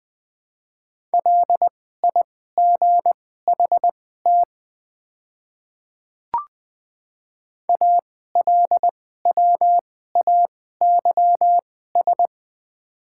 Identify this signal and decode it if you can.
Morse code, heard as a keyed tone, about 20 wpm, spaced as for 14 wpm, 700 Hz. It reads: LIGHT ALWAYS
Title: 2x Character Spacing - Top 300 Words - 20wpm